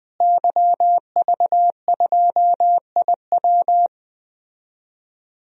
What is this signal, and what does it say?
Morse code, heard as a keyed tone, about 21 wpm, 700 Hz. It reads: YV2IW
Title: DXCC-100 Call Signs - 5 Characters - Repeated 2x - 20wpm